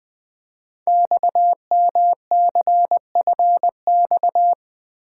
Morse code, heard as a keyed tone, about 20 wpm, 700 Hz. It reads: XMCFX
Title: Code Groups - 5 Letters - 20wpm